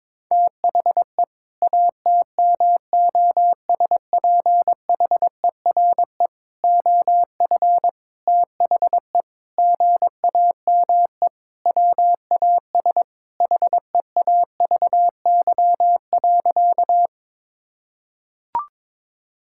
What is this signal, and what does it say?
Morse code, heard as a keyed tone, about 22 wpm, 700 Hz. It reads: THE ATMOSPHERE OF THE GAME WAS HEAVY.
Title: Sentences from Top 1600 Words 22wpm